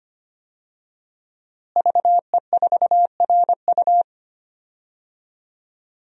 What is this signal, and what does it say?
Morse code, heard as a keyed tone, about 25 wpm, 700 Hz. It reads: VE4RU
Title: DXCC-100 Call Signs - 5 Characters - Repeated 3x - 25wpm